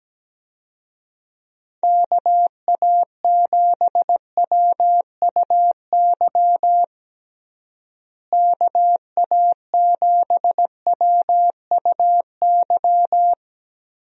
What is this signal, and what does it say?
Morse code, heard as a keyed tone, about 17 wpm, 700 Hz. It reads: KA7WUY KA7WUY
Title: USA Call Signs - 6 Characters - Repeated 2x - 17wpm